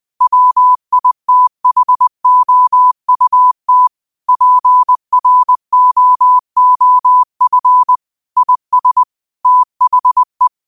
Morse code, heard as a keyed tone, about 20 wpm, 1000 Hz. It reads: WITHOUT PROOF IS THE